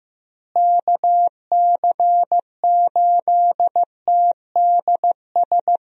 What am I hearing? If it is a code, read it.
Morse code, heard as a keyed tone, about 15 wpm, 700 Hz. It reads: KC8TDS
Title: USA Call Signs - 6 Characters - Repeated 3x - 15wpm